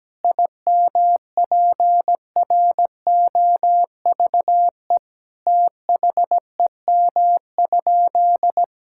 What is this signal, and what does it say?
Morse code, heard as a keyed tone, about 17 wpm, 700 Hz. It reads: IMPROVE THEM?
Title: Sentences from Top 1000 Words - Review Part 5 - 17wpm